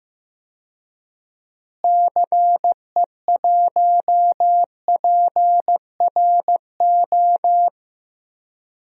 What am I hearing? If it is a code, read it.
Morse code, heard as a keyed tone, about 15 wpm, 700 Hz. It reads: CE1PRO
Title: DXCC-100 Call Signs - 6 Characters - Repeated 2x - 15wpm